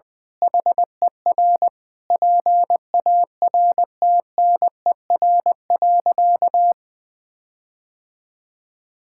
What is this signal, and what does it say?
Morse code, heard as a keyed tone, about 21 wpm, 700 Hz. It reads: HER PARTNER.
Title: Sentences with 8-Word Phrases - Spoken Beginning - 20wpm